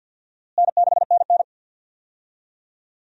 Morse code, heard as a keyed tone, about 50 wpm, 700 Hz. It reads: N6NN
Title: USA Call Signs - 4 Characters - Repeated 2x - 50wpm